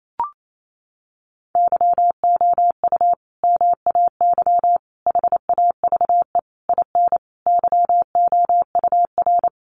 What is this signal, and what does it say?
Morse code, heard as a keyed tone, about 28 wpm, 700 Hz. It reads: YOU MAY HAVE IN YOUR
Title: Sentences from Top 1400 Words 28wpm